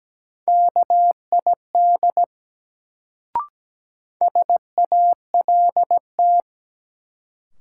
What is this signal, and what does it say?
Morse code, heard as a keyed tone, about 17 wpm, 700 Hz. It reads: KID SALT
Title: Sets of 5 Words - New Word Drill - 17wpm